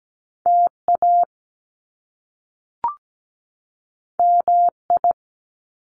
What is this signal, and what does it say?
Morse code, heard as a keyed tone, about 17 wpm, 700 Hz. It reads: TA MI
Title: Top Two Letter Combinations in Words 17wpm